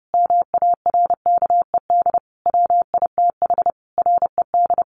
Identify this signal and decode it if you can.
Morse code, heard as a keyed tone, about 30 wpm, 700 Hz. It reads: MARKED WITH RED
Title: A Sound Of Thunder 30wpm